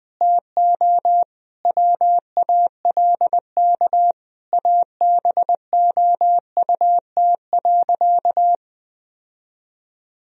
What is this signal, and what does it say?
Morse code, heard as a keyed tone, about 21 wpm, 700 Hz. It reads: TO WALK ABOUT.
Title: Sentences from Top 1000 Words - Review Part 1 - 20wpm